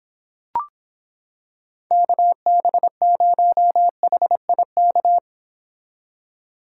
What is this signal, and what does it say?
Morse code, heard as a keyed tone, about 26 wpm, 700 Hz. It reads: KB0HIK